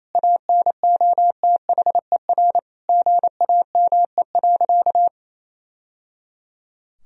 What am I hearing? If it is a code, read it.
Morse code, heard as a keyed tone, about 27 wpm, 700 Hz. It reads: ANOTHER GAME.
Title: Sentences from Top 1000 Words - Review Part 3 - 28wpm